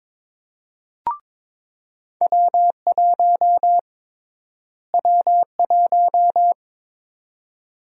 Morse code, heard as a keyed tone, about 22 wpm, 700 Hz. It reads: W1 W1
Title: USA Call Sign Prefixes - Repeated 2x - 22wpm